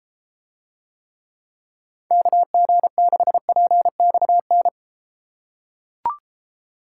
Morse code, heard as a keyed tone, about 32 wpm, 700 Hz. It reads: KG6PXN